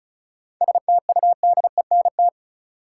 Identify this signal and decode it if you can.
Morse code, heard as a keyed tone, about 36 wpm, 700 Hz. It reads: STUDENT
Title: Sentences from Top 1000 Words - New Word Drill - 35wpm